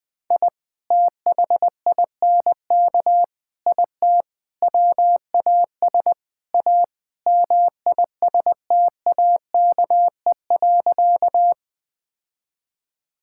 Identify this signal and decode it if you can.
Morse code, heard as a keyed tone, about 20 wpm, 700 Hz. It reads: I THINK IT WAS A MISTAKE.
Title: Sentences with 5-Word Phrases - 20wpm